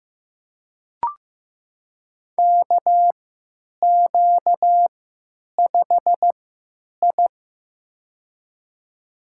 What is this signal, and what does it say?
Morse code, heard as a keyed tone, about 15 wpm, spaced as for 9 wpm, 700 Hz. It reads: KQ5I
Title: USA Call Signs - 3x Character Spacing - 15wpm